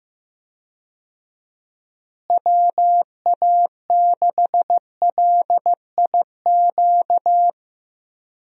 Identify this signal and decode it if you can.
Morse code, heard as a keyed tone, about 15 wpm, 700 Hz. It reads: WA6LIQ
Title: USA Call Signs - 6 Characters - Repeated 2x - 15wpm